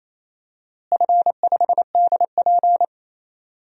Morse code, heard as a keyed tone, about 27 wpm, 700 Hz. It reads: F5DP
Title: DXCC-100 Call Signs - 4 Characters - Repeated 2x - 28wpm